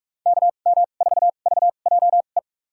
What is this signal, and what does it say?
Morse code, heard as a keyed tone, about 45 wpm, spaced as for 32 wpm, 700 Hz. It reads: KM4VJE
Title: USA Call Signs - 6 Characters - 2x Character Spacing - 45wpm